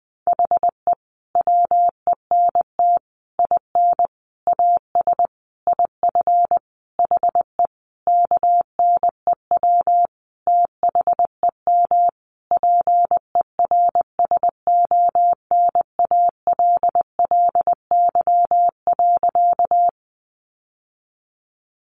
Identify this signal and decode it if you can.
Morse code, heard as a keyed tone, about 20 wpm, 700 Hz. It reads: HE WENT IN AS IF HE KNEW THEM PERSONALLY.
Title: Sentences from Top 1400 Words 20wpm